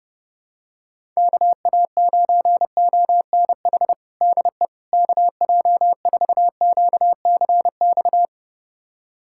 Morse code, heard as a keyed tone, about 31 wpm, 700 Hz. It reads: KA9ONH DE KJ4QCX
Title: USA Callsign DE Callsign - 6 Characters - 30wpm